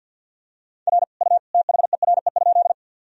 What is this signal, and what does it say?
Morse code, heard as a keyed tone, about 50 wpm, 700 Hz. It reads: R U THERE?
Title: Sentences from Top 100 Words and Abbreviations 50wpm